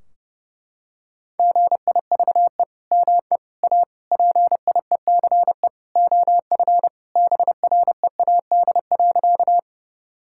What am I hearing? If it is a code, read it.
Morse code, heard as a keyed tone, about 32 wpm, 700 Hz. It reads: GIVE ME A PIECE OF BREAD.
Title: Sentences from Top 1600 Words 30wpm